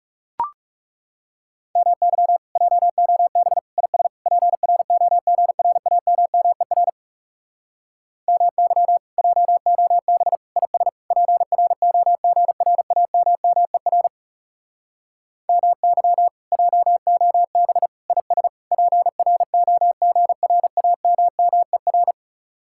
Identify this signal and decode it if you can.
Morse code, heard as a keyed tone, about 40 wpm, 700 Hz. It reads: MY JOB IS PROGRAMMER MY JOB IS PROGRAMMER MY JOB IS PROGRAMMER